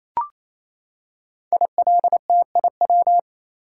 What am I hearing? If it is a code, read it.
Morse code, heard as a keyed tone, about 27 wpm, 700 Hz. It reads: ILTIW
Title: Code Groups - 5 Letters - 28wpm